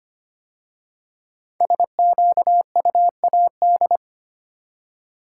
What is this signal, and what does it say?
Morse code, heard as a keyed tone, about 25 wpm, 700 Hz. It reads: SQUAD
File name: 5 Letter Words - First 3 Letters Given - 25wpm